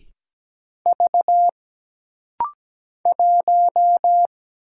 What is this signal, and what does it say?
Morse code, heard as a keyed tone, about 17 wpm, 700 Hz. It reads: V 1
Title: Single Letter-Number Alt - Rapid-Fire 17wpm